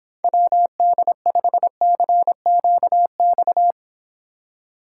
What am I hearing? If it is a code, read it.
Morse code, heard as a keyed tone, about 26 wpm, 700 Hz. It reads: WD5CQX